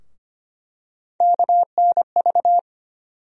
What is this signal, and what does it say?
Morse code, heard as a keyed tone, about 25 wpm, 700 Hz. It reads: KNV
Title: USA Call Sign Postfixes - 3 Letters 25wpm